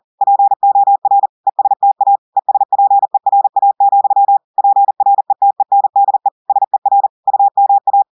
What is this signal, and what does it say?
Morse code, heard as a keyed tone, about 40 wpm, 800 Hz. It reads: POR ESTA ESPERA, PRETENDE SER UMA
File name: audios / portugais / portugais texte en clair 1 vitesse 40 mots minutes